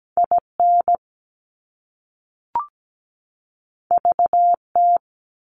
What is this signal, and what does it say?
Morse code, heard as a keyed tone, about 17 wpm, 700 Hz. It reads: IN VT
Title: US States 17wpm